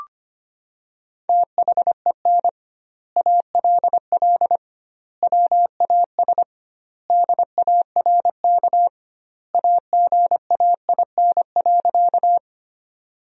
Morse code, heard as a keyed tone, about 25 wpm, 700 Hz. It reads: THEN ALL WAS DARK AGAIN.
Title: 2x Word-Spacing - Sentences from Top 500 Words - 25wpm